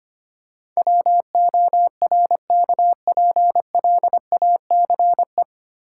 Morse code, heard as a keyed tone, about 25 wpm, 700 Hz. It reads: WORKPLACE
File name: practice / 25wpm / 9 Letter Words - First 3 Letters Given - 25wpm